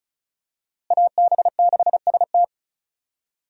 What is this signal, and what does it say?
Morse code, heard as a keyed tone, about 36 wpm, 700 Hz. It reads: AB6ST
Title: USA Call Signs - 5 Characters - Repeated 3x - 35wpm